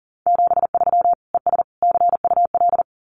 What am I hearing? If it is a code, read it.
Morse code, heard as a keyed tone, about 38 wpm, 700 Hz. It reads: 73 ES CUL
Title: Common QSO Phrases Encore 40wpm